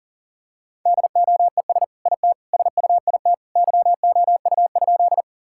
Morse code, heard as a keyed tone, about 40 wpm, 700 Hz. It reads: DOES IT SUIT YOU?
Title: Sentences from Top 1600 Words 40wpm